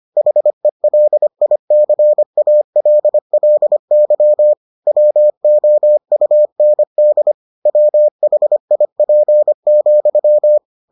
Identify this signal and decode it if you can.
Morse code, heard as a keyed tone, about 25 wpm, 600 Hz. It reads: HELICALLY WOUND WHIP,